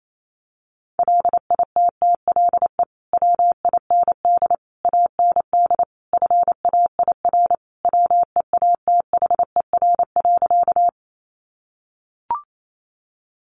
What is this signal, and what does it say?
Morse code, heard as a keyed tone, about 28 wpm, 700 Hz. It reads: LITTLE WIND AND FAIR WEATHER.
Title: Sentences from Top 1300 Words 28wpm